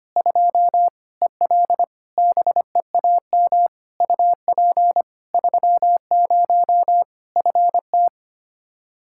Morse code, heard as a keyed tone, about 25 wpm, 700 Hz. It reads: 2 EL BEAM UP 30 FT
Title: Practice QSO - Exchange 2 - Info - 25wpm